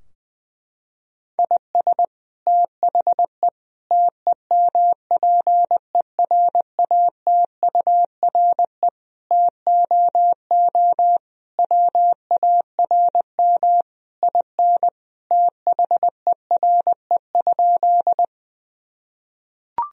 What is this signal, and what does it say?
Morse code, heard as a keyed tone, about 19 wpm, 700 Hz. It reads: IS THE TEMPERATURE TOO WARM IN THERE?
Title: Sentences from Top 1000 Words - Review Part 4 - 20wpm